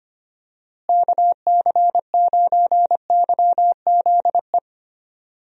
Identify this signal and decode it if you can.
Morse code, heard as a keyed tone, about 25 wpm, 700 Hz. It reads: KC9YZE
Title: USA Call Signs - 6 Characters - Repeated 3x - 25wpm